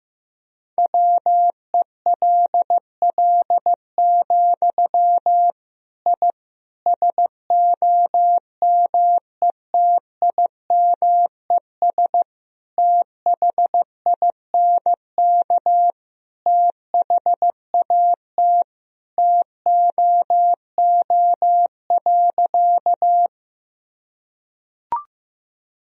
Morse code, heard as a keyed tone, about 15 wpm, 700 Hz. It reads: WELL, I SOMETIMES THINK THAT TOO.
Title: Sentences from Top 1000 Words - Review Part 3 - 15wpm